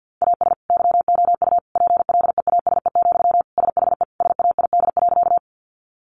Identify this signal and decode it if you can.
Morse code, heard as a keyed tone, about 50 wpm, 700 Hz. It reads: AS YOU PLEASE, SHE SAID.
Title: Sentences from Top 1500 Words 50wpm